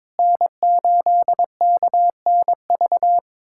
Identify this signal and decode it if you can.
Morse code, heard as a keyed tone, about 22 wpm, 700 Hz. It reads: N8KNV
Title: USA Callsign DE Callsign - 5 Characters - 22wpm